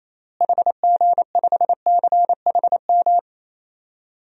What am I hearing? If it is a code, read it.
Morse code, heard as a keyed tone, about 28 wpm, 700 Hz. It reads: HG5CHM
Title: DXCC-100 Call Signs - 6 Characters - Repeated 3x - 28wpm